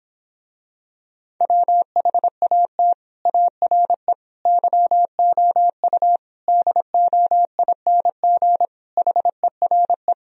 Morse code, heard as a keyed tone, about 26 wpm, 700 Hz. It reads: WHAT ARE YOU DOING HERE